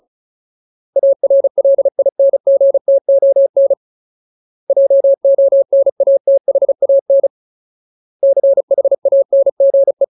Morse code, heard as a keyed tone, about 34 wpm, 550 Hz. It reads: ARLINGTON JONATHAN CHANGE
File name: Ultimate Word List / 35WPM Ultimate